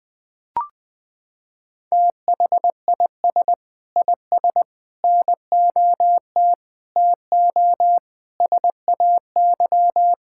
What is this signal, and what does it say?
Morse code, heard as a keyed tone, about 20 wpm, 700 Hz. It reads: THIS IS NOT TO SAY
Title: Sets of 6 Words - 20wpm